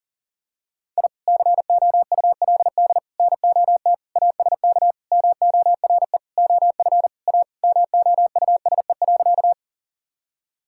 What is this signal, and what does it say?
Morse code, heard as a keyed tone, about 37 wpm, 700 Hz. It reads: I COULD NOT ASK MORE OF A MOUSE.
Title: Sentences from Top 1000 Words - Review Part 5 - 40wpm